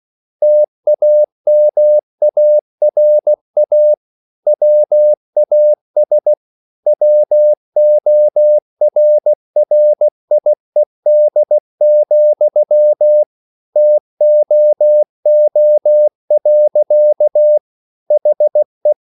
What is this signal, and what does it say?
Morse code, heard as a keyed tone, about 16 wpm, 600 Hz. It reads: TAMARA WAS WORRIED, TOO. HE